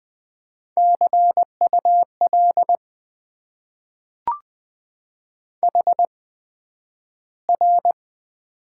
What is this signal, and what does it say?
Morse code, heard as a keyed tone, about 20 wpm, spaced as for 11 wpm, 700 Hz. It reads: CUL HR
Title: Top 90 QSO Elements - 8x Character Spacing - 20wpm